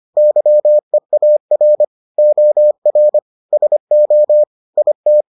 Morse code, heard as a keyed tone, about 25 wpm, 600 Hz. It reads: YEAR OR SO IT